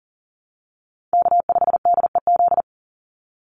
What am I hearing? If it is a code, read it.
Morse code, heard as a keyed tone, about 40 wpm, 700 Hz. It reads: K5DEZ
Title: NA SOTA Chaser Callsigns 40wpm